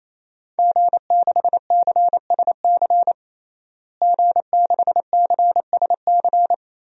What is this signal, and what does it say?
Morse code, heard as a keyed tone, about 28 wpm, 700 Hz. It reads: G6CSC G6CSC
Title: DXCC-100 Call Signs - 5 Characters - Repeated 3x - 28wpm